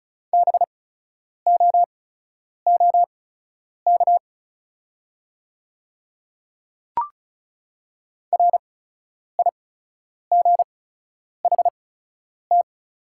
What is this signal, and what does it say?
Morse code, heard as a keyed tone, about 34 wpm, spaced as for 10 wpm, 690 Hz. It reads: BOOK RIGHT
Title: 8x Character Spacing - Top 300 Words - 35wpm